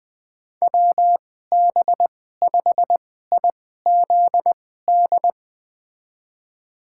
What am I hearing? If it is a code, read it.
Morse code, heard as a keyed tone, about 20 wpm, spaced as for 14 wpm, 700 Hz. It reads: WB5IZD
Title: USA Call Signs - 2x Character Spacing - 20wpm